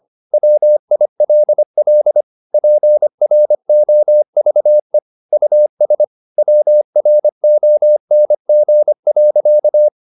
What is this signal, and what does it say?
Morse code, heard as a keyed tone, about 25 wpm, 600 Hz. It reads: WILL PROVE US WRONG.
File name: the old man and the sea part 1 25wpm